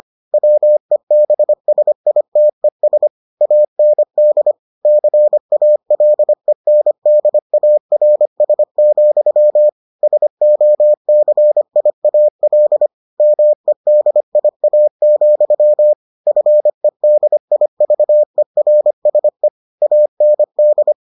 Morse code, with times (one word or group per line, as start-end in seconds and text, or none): 0.3-3.1 WEBSITES
3.4-4.5 AND
4.9-9.7 CALENDARS,
10.0-12.9 SOCIAL
13.2-15.9 MEDIA,
16.3-19.5 FEDIVERSE
19.8-20.9 AND